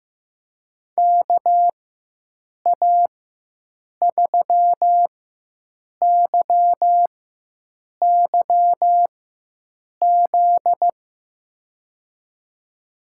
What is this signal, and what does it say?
Morse code, heard as a keyed tone, about 15 wpm, spaced as for 7 wpm, 700 Hz. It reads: KA3YYZ
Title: USA Call Signs - 6 Characters - 4x Character Spacing - 15wpm